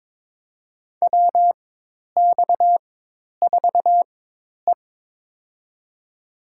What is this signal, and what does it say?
Morse code, heard as a keyed tone, about 22 wpm, spaced as for 10 wpm, 700 Hz. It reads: WX4E
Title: USA Call Signs - 4x Character Spacing - 22wpm